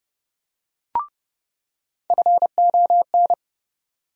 Morse code, heard as a keyed tone, about 30 wpm, 700 Hz. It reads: FON
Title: USA Call Sign Postfixes - 3 Letters 30wpm